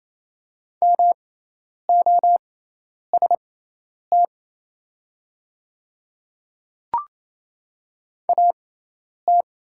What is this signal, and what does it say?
Morse code, heard as a keyed tone, about 27 wpm, spaced as for 10 wpm, 700 Hz. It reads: MOST AT
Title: 6x Character Spacing - Top 100 Words - 28wpm